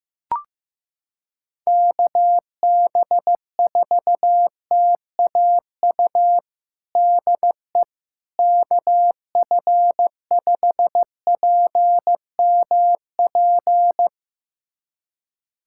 Morse code, heard as a keyed tone, about 15 wpm, 700 Hz. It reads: KB4TAU DE KF5PMP
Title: USA Callsign DE Callsign - 6 Characters - 15wpm